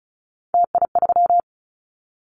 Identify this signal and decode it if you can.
Morse code, heard as a keyed tone, about 36 wpm, 700 Hz. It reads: TI3
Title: Callsign Prefixes for Top 100 DXCC - 35wpm